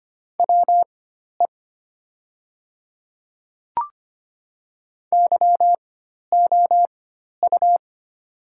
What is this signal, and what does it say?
Morse code, heard as a keyed tone, about 25 wpm, spaced as for 12 wpm, 700 Hz. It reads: WE YOU
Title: 4x Character Spacing - Top 100 Words - 25wpm